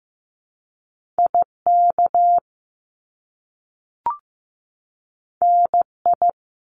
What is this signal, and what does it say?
Morse code, heard as a keyed tone, about 15 wpm, 700 Hz. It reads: IK NI
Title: Top Two Letter Combinations in Words 15wpm